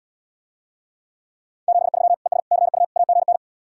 Morse code, heard as a keyed tone, about 77 wpm, 700 Hz. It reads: 73 ES CU AGN